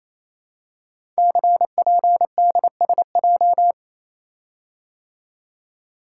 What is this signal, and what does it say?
Morse code, heard as a keyed tone, about 27 wpm, 700 Hz. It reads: CPDSJ
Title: Code Groups - 5 Letter:Numbers - 28wpm